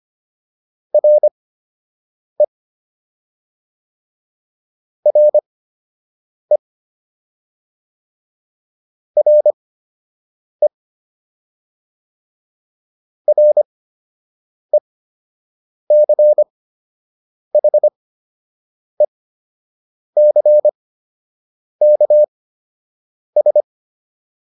Morse code, heard as a keyed tone, about 25 wpm, spaced as for 7 wpm, 600 Hz. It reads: RE RE RE RECHECKS
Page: fundamental-practice-files